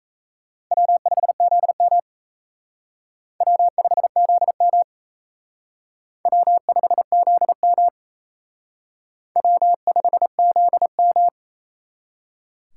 Morse code, heard as a keyed tone, about 34 wpm, 700 Hz. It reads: W5ZM W5ZM W5ZM W5ZM
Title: Speed Racing - USA 1x2 Call Signs 28wpm